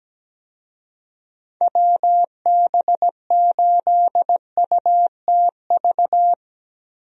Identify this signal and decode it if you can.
Morse code, heard as a keyed tone, about 17 wpm, 700 Hz. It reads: WB8UTV
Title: USA Call Signs - 6 Characters - 17wpm